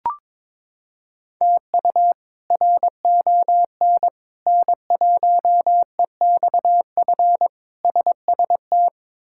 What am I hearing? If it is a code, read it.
Morse code, heard as a keyed tone, about 21 wpm, 700 Hz. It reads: TU RON N1EXF SST
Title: K1USN SST QSO Practice - 22wpm